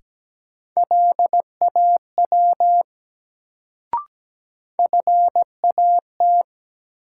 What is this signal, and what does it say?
Morse code, heard as a keyed tone, about 17 wpm, 700 Hz. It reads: LAW FAT
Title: Sentences from Top 600 Words - New Word Drill - 17wpm